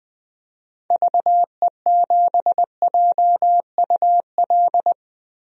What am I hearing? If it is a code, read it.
Morse code, heard as a keyed tone, about 20 wpm, 700 Hz. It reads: VE7JUL
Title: DXCC-100 Call Signs - 6 Characters - Repeated 3x - 20wpm